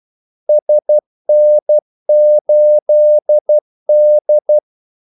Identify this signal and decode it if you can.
Morse code, heard as a keyed tone, about 12 wpm, 600 Hz. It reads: SN8D
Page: 2 ekstra ordmellemrum mellem kaldesignalerne